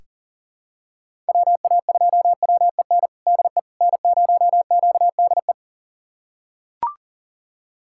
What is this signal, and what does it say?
Morse code, heard as a keyed tone, about 40 wpm, 700 Hz. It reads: WA2WEN DE N0QDE
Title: USA Callsign DE Callsign 40wpm